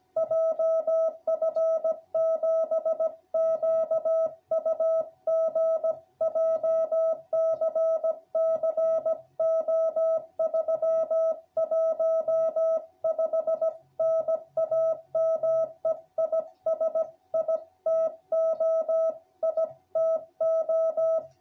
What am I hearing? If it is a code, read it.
Morse code, heard as a keyed tone, about 16 wpm, 660 Hz. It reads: JF7QUG JCC O315 NAME IS ITO ITO